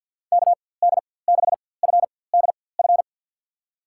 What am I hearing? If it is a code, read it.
Morse code, heard as a keyed tone, about 52 wpm, spaced as for 23 wpm, 700 Hz. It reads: KD6FDF
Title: USA Call Signs - 6 Characters - 4x Character Spacing - 50wpm